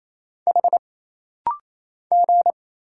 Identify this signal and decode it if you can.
Morse code, heard as a keyed tone, about 28 wpm, 700 Hz. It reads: H G
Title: Single Letter-Number - Rapid-Fire 28wpm